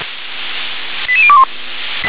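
Five-tone selective call (ZVEI), heard as quick runs of five tones, tones of about 70 ms. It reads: 88021
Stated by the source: ZVEI five-tone selective call received by SDR.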